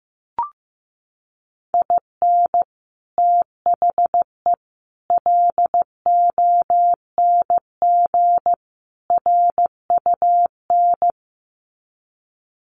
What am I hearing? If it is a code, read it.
Morse code, heard as a keyed tone, about 15 wpm, 700 Hz. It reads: IN THE LONG RUN
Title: Sets of 4 Words 15wpm